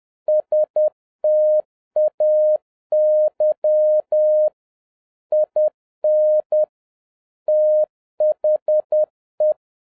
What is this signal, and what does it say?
Morse code, heard as a keyed tone, about 10 wpm, 610 Hz. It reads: STAY IN THE
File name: 2011-07-mp3 / n56 BACK FROM A BIG ACCIDENT 10wpm